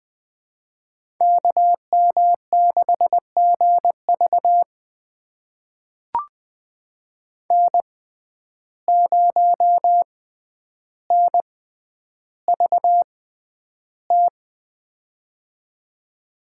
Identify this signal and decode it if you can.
Morse code, heard as a keyed tone, about 20 wpm, spaced as for 10 wpm, 700 Hz. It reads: KM6GV N0NVT
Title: USA Call Signs - 5 Characters - 6x Character Spacing - 20wpm